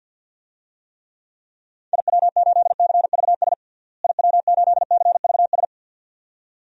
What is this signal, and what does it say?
Morse code, heard as a keyed tone, about 52 wpm, 700 Hz. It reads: IW8CVS IW8CVS
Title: DXCC-100 Call Signs - 6 Characters - Repeated 2x - 50wpm